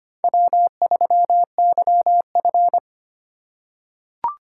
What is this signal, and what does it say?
Morse code, heard as a keyed tone, about 25 wpm, 700 Hz. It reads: W3YF
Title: USA Call Signs - 4x Character Spacing - 25wpm